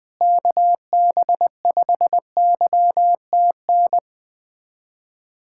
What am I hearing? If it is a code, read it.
Morse code, heard as a keyed tone, about 21 wpm, 700 Hz. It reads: KB5YTN
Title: USA Callsign DE Callsign 20wpm